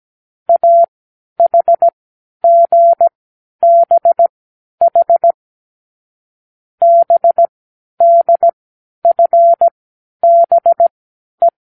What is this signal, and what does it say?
Morse code, heard as a keyed tone, about 17 wpm, spaced as for 10 wpm, 680 Hz. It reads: AHGBH BDFBE
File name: k7qo / Lesson12(A-H)